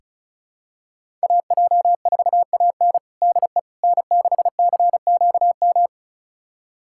Morse code, heard as a keyed tone, about 36 wpm, 700 Hz. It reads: AJ4AN DE N6CQM
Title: USA Callsign DE Callsign - 5 Characters - 35wpm